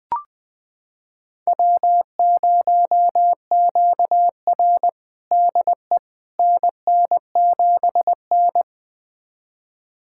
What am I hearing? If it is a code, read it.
Morse code, heard as a keyed tone, about 21 wpm, 700 Hz. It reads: W0QR DE NN7N
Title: USA Callsign DE Callsign - 4 Characters - 20wpm